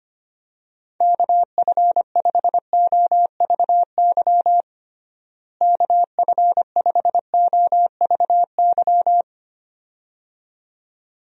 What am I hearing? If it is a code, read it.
Morse code, heard as a keyed tone, about 25 wpm, 700 Hz. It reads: KF5OVY KF5OVY
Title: USA Call Signs - 6 Characters - Repeated 2x - 25wpm